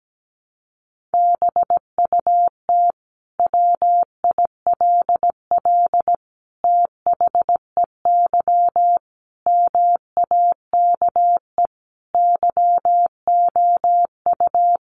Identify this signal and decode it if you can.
Morse code, heard as a keyed tone, about 17 wpm, 700 Hz. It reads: BUT WILL THEY MAKE YOU
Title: Sentences from Top 1000 Words 17wpm